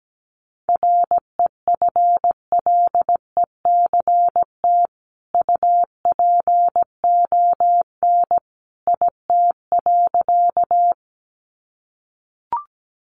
Morse code, heard as a keyed tone, about 17 wpm, 700 Hz. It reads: REFLECT UPON IT.
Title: Sentences from Top 1500 Words 17wpm